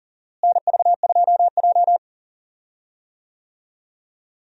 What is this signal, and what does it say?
Morse code, heard as a keyed tone, about 40 wpm, 700 Hz. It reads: NV2J
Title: USA Call Signs - 4 Characters - Repeated 3x - 40wpm